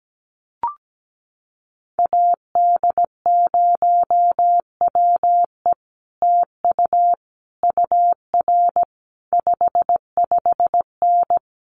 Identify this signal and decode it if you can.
Morse code, heard as a keyed tone, about 17 wpm, 700 Hz. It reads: AD0WE TU UR 55N
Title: Common QSO Phrases Encore 17wpm